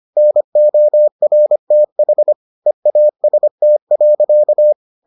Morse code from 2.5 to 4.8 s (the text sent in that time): EAST.